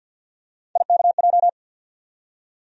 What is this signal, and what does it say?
Morse code, heard as a keyed tone, about 48 wpm, 700 Hz. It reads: IKJ
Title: USA Call Sign Postfixes - 3 Letters 50wpm